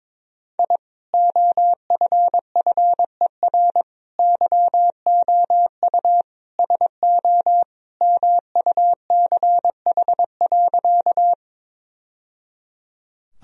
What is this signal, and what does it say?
Morse code, heard as a keyed tone, about 22 wpm, 700 Hz. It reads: I OFFER YOU SO MUCH.
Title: Sentences from Top 1000 Words - Review Part 4 - 22wpm